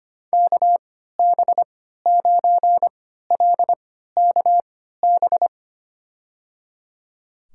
Morse code, heard as a keyed tone, about 25 wpm, spaced as for 14 wpm, 700 Hz. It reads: KB9LKB
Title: USA Call Signs - 6 Characters - 3x Character Spacing - 25wpm